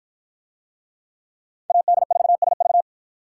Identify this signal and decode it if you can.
Morse code, heard as a keyed tone, about 50 wpm, 700 Hz. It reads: AD4SV